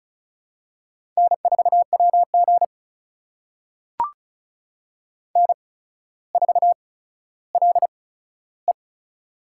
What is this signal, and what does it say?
Morse code, heard as a keyed tone, about 36 wpm, spaced as for 15 wpm, 700 Hz. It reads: N4WG N4LE
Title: USA Call Signs - 4 Characters - 8x Character Spacing - 35wpm